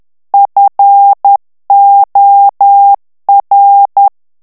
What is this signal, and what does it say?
Morse code, heard as a keyed tone, about 10 wpm, 800 Hz. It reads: FOR
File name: ANGLAIS texte clair 3 vitesse 10 mots minute